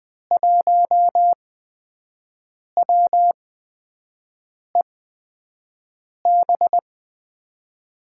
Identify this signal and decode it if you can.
Morse code, heard as a keyed tone, about 20 wpm, spaced as for 5 wpm, 700 Hz. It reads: 1WEB
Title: USA Call Signs - 6 Characters - 8x Character Spacing - 20wpm